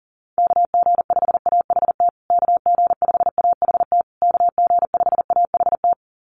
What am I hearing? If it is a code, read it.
Morse code, heard as a keyed tone, about 40 wpm, 700 Hz. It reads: KG5AHT KG5AHT KG5AHT
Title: USA Call Signs Repeated 3x 40wpm